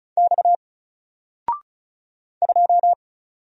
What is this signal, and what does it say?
Morse code, heard as a keyed tone, about 36 wpm, 700 Hz. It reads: X 2
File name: Single-Letter-Number-35wpm